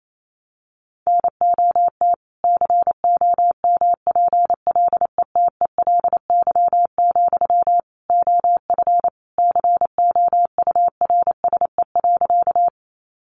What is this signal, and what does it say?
Morse code, heard as a keyed tone, about 28 wpm, 700 Hz. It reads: NOT COMPLETELY, OF COURSE.
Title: Sentences from Top 1100 Words 28wpm